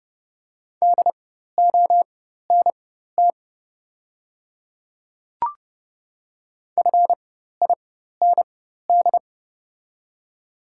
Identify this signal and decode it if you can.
Morse code, heard as a keyed tone, about 30 wpm, spaced as for 14 wpm, 700 Hz. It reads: DONT FIND
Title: 4x Character Spacing - Top 300 Words - 30wpm